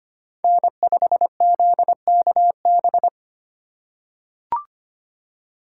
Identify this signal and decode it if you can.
Morse code, heard as a keyed tone, about 25 wpm, 700 Hz. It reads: N5ZKB
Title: USA Call Signs - 2x Character Spacing - 25wpm